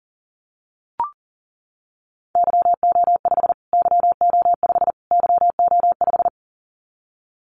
Morse code, heard as a keyed tone, about 40 wpm, 700 Hz. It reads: YO5 YO5 YO5
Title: Callsign Prefixes for Top 25 DXCC - Repeated 3x - 40wpm